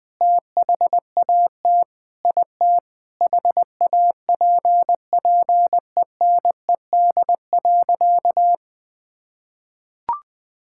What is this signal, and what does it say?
Morse code, heard as a keyed tone, about 21 wpm, 700 Hz. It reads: THAT IT HAPPENED.
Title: Sentences with 8-Word Phrases - Spoken Beginning - 20wpm